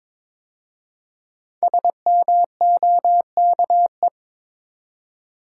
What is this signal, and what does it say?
Morse code, heard as a keyed tone, about 22 wpm, 700 Hz. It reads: SMOKE
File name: 5 Letter Words - First 3 Letters Given - 22wpm